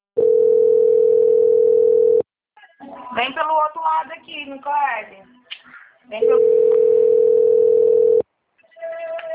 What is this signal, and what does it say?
Ringback tone (North America), 440 + 480 Hz, with speech.
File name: out- - - - - .14243